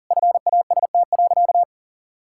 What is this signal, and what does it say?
Morse code, heard as a keyed tone, about 40 wpm, 700 Hz. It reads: FAST.
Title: Sentences from Top 1600 Words 40wpm